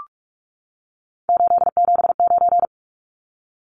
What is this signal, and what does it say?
Morse code, heard as a keyed tone, about 44 wpm, 700 Hz. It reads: 879
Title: Numbers - 3-Digit 45wpm